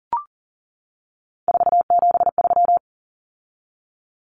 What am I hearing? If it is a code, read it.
Morse code, heard as a keyed tone, about 40 wpm, 700 Hz. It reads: 473
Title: Numbers - 3-Digit 40wpm